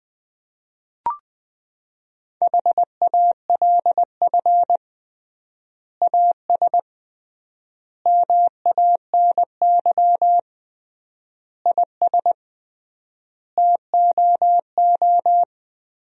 Morse code, heard as a keyed tone, about 21 wpm, 700 Hz. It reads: HALF AS MANY IS TOO
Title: 3x Word-Spacing - Sentences from Top 400 Words - 20wpm